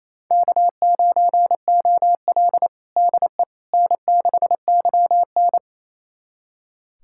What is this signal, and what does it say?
Morse code, heard as a keyed tone, about 28 wpm, 700 Hz. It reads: K9OL DE N6YN
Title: USA Callsign DE Callsign - 4 Characters - 28wpm